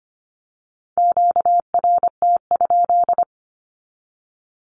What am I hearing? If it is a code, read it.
Morse code, heard as a keyed tone, about 25 wpm, 700 Hz. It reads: QRT?
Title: Q-Codes 25wpm